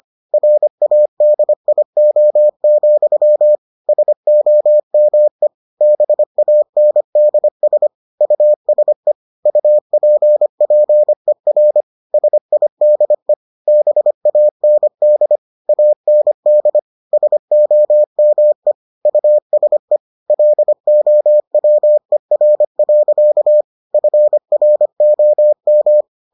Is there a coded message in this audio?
Yes: Morse code from 0.3 to 26.0 s, heard as a keyed tone, about 25 wpm, 600 Hz. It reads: RADIO, SOME BANDS USE UPPER SIDE BAND AND SOME USE LOWER. FROM